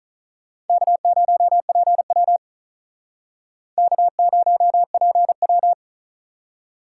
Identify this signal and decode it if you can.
Morse code, heard as a keyed tone, about 46 wpm, 700 Hz. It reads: K0PW K0PW